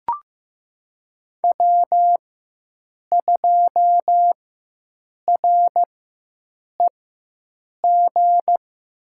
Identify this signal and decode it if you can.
Morse code, heard as a keyed tone, about 16 wpm, spaced as for 7 wpm, 700 Hz. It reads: W2REG
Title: USA Call Signs - 5 Characters - 4x Character Spacing - 15wpm